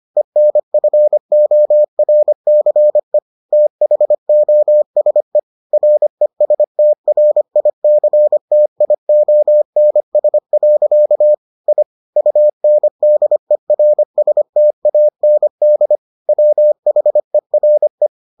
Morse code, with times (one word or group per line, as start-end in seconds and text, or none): 0.2-3.2 ENFORCE
3.5-5.4 THOSE
5.7-11.3 RESTRICTIONS.
11.7-11.8 I
12.2-16.0 UNDERSTAND
16.3-18.1 WHERE